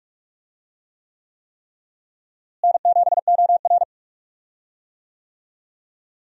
Morse code, heard as a keyed tone, about 44 wpm, 700 Hz. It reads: N7OR
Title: USA Call Signs - 4 Characters - Repeated 3x - 45wpm